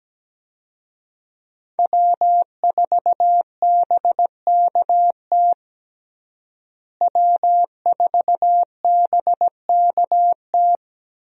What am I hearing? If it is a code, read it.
Morse code, heard as a keyed tone, about 17 wpm, 700 Hz. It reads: W4BKT W4BKT
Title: USA Call Signs - 5 Characters - Repeated 2x - 17wpm